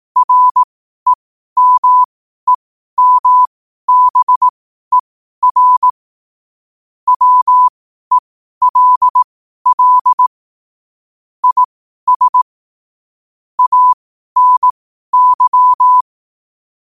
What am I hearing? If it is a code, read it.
Morse code, heard as a keyed tone, about 18 wpm, spaced as for 12 wpm, 1000 Hz. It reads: REMEMBER WELL IS ANY
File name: qotdfiles / 12wpm